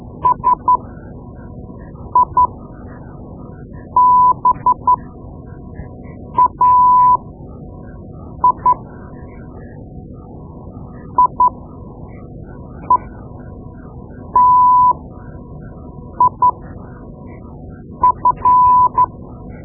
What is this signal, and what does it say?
Morse code, heard as a keyed tone, about 12 wpm, spaced as for 5 wpm, 1000 Hz. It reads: SIBAI IETIF
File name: Morse Lessons / Morse Lessons / Tape-2B